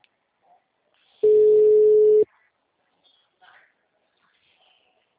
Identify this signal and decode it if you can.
Ringback tone (Europe, 425 Hz), with speech.